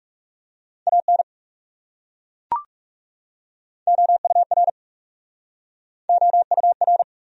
Morse code, heard as a keyed tone, about 39 wpm, 700 Hz. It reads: AN OUR OUR